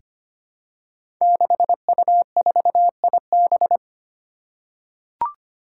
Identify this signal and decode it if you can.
Morse code, heard as a keyed tone, about 25 wpm, 700 Hz. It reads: 6U4IB
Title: Five Challenging Letter-Number Combinations 25wpm